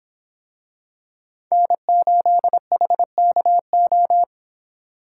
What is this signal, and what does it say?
Morse code, heard as a keyed tone, about 24 wpm, 700 Hz. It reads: N8HKO